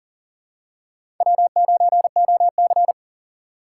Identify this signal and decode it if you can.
Morse code, heard as a keyed tone, about 40 wpm, 700 Hz. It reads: W9OC
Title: USA Call Signs - 4x Character Spacing - 40wpm